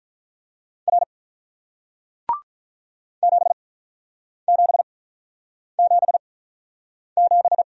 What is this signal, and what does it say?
Morse code, heard as a keyed tone, about 48 wpm, spaced as for 9 wpm, 700 Hz. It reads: R 7777